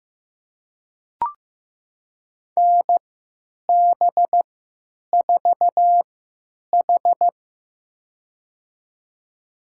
Morse code, heard as a keyed tone, about 15 wpm, spaced as for 9 wpm, 700 Hz. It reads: NB4H
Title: USA Call Signs - 4 Characters - 3x Character Spacing - 15wpm